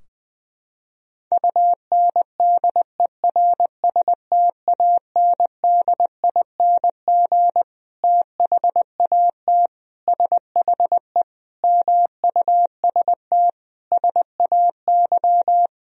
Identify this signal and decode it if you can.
Morse code, heard as a keyed tone, about 20 wpm, 700 Hz. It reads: UNDERSTANDING THAT SHE MUST SAY
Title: Sentences from Top 1000 Words - Review Part 3 - 20wpm